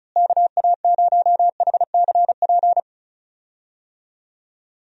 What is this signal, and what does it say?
Morse code, heard as a keyed tone, about 34 wpm, 700 Hz. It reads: KA0HCP
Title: USA Call Signs - 6 Characters - Repeated 3x - 35wpm